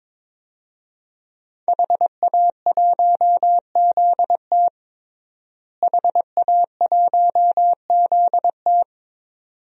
Morse code, heard as a keyed tone, about 22 wpm, 700 Hz. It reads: HA1ZT HA1ZT
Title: DXCC-100 Call Signs - 5 Characters - Repeated 2x - 22wpm